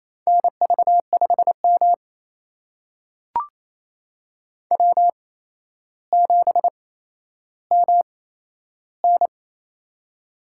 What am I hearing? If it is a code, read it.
Morse code, heard as a keyed tone, about 28 wpm, spaced as for 12 wpm, 700 Hz. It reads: NV5M W7MN
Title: USA Call Signs - 4 Characters - 8x Character Spacing - 28wpm